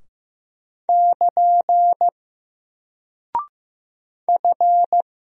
Morse code, heard as a keyed tone, about 15 wpm, 700 Hz. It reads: <KN> F
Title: Single Character (Letter:Number:Punctuation:Prosign) Alt 15wpm